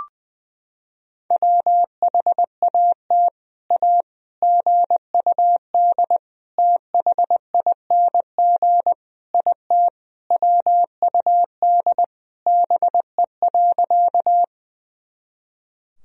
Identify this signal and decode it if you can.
Morse code, heard as a keyed tone, about 19 wpm, 700 Hz. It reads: WHAT A GUD THING IT WUD BE.
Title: Sentences from Top 200 Words and Abbreviations 20wpm